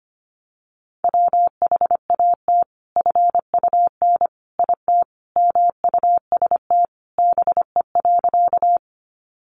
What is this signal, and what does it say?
Morse code, heard as a keyed tone, about 25 wpm, 700 Hz. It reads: WHAT FUN IT MUST BE.
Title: Sentences from Top 700 Words 25wpm